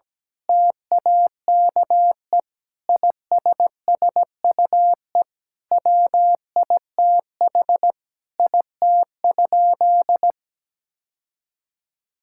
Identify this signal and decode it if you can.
Morse code, heard as a keyed tone, about 17 wpm, 700 Hz. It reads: TAKE ISSUE WITH IT?
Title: Sentences from Top 1000 Words - Review Part 6 - 17wpm